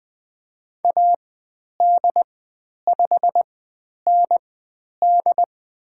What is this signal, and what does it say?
Morse code, heard as a keyed tone, about 20 wpm, spaced as for 10 wpm, 700 Hz. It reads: AD5ND
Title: 102 Lesson 17g - Letter u - Call Signs 20wpm